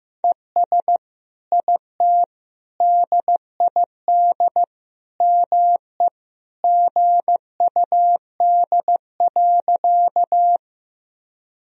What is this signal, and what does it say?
Morse code, heard as a keyed tone, about 15 wpm, 700 Hz. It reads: ES IT DID ME GUD.
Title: Sentences from Top 200 Words and Abbreviations 15wpm